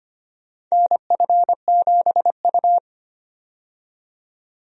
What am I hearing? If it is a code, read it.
Morse code, heard as a keyed tone, about 25 wpm, 700 Hz. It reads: NF7U
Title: USA Call Signs - 4 Characters - 25wpm